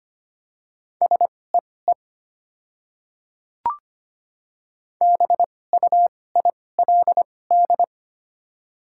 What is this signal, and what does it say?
Morse code, heard as a keyed tone, about 25 wpm, spaced as for 18 wpm, 690 Hz. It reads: SEE BUILD
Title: 2x Character Spacing - Top 200 Words - 25wpm